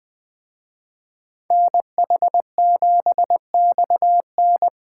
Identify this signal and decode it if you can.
Morse code, heard as a keyed tone, about 20 wpm, 700 Hz. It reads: NH7XN
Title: Speed Racing - USA 2x2 Call Signs 17wpm